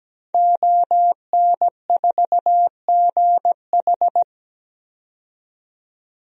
Morse code, heard as a keyed tone, about 17 wpm, 700 Hz. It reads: ON4GH
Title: DXCC-100 Call Signs - 5 Characters - 17wpm